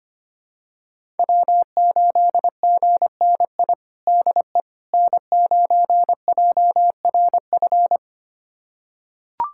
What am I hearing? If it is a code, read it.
Morse code, heard as a keyed tone, about 25 wpm, 700 Hz. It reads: W8GNI DE N9JRF
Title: USA Callsign DE Callsign - 5 Characters - 25wpm